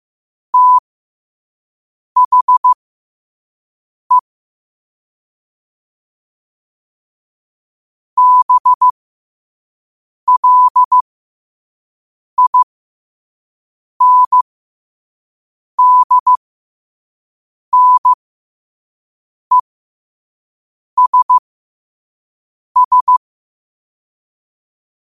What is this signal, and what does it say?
Morse code, heard as a keyed tone, about 15 wpm, spaced as for 5 wpm, 1000 Hz. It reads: THE BLINDNESS
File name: qotdfiles / 5wpm